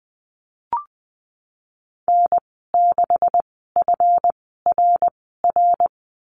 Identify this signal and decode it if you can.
Morse code, heard as a keyed tone, about 19 wpm, spaced as for 14 wpm, 700 Hz. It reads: N6FRR
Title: 147 Lesson 24g - Letter y - Call Signs 20wpm